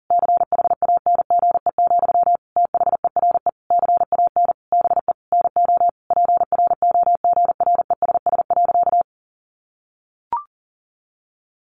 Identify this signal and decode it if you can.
Morse code, heard as a keyed tone, about 37 wpm, 700 Hz. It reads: CHANGE, THERE CAN BE NO PROGRESS.
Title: Sentences from Top 1400 Words 40wpm